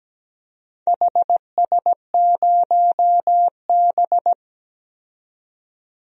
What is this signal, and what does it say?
Morse code, heard as a keyed tone, about 17 wpm, 700 Hz. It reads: HS0B
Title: DXCC-100 Call Signs - 4 Characters - 17wpm